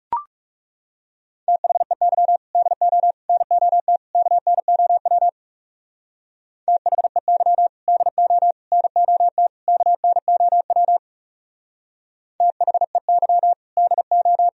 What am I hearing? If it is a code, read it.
Morse code, heard as a keyed tone, about 39 wpm, 700 Hz. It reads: THEY DO NOT KNOW THEY DO NOT KNOW THEY DO